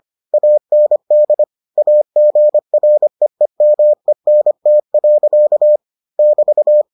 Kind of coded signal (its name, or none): Morse code